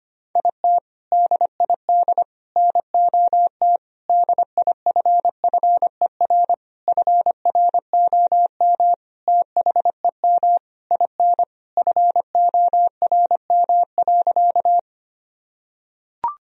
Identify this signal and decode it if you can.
Morse code, heard as a keyed tone, about 25 wpm, 700 Hz. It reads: IT DID NOT DIFFER FROM THEM IN FORM.
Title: Sentences from Top 1000 Words - Review Part 1 - 25wpm